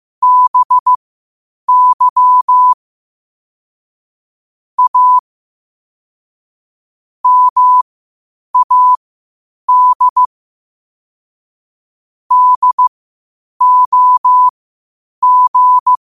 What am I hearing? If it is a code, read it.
Morse code, heard as a keyed tone, about 15 wpm, spaced as for 8 wpm, 1000 Hz. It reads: BY A MAD DOG
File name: qotdfiles / 8wpm